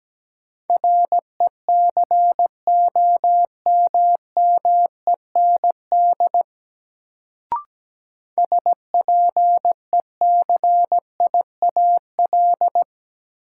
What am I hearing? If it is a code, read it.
Morse code, heard as a keyed tone, about 17 wpm, 700 Hz. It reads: RECOMMEND SPECIAL
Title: Sentences from Top 1200 Words - New Word Drill Alt - 17wpm